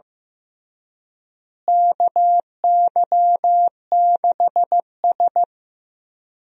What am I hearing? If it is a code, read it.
Morse code, heard as a keyed tone, about 15 wpm, 700 Hz. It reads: KY6S
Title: USA Call Signs - 4 Characters - Repeated 3x - 15wpm